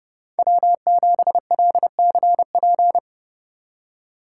Morse code, heard as a keyed tone, about 37 wpm, 700 Hz. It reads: W7LCP
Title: Speed Racing - USA 1x3 Call Signs 30wpm